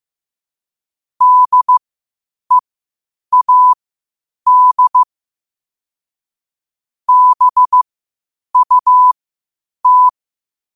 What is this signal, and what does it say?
Morse code, heard as a keyed tone, about 15 wpm, spaced as for 8 wpm, 1000 Hz. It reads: DEAD BUT
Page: Quotes for Thu, 14 Aug 2025 in Morse Code at 8 words per minute.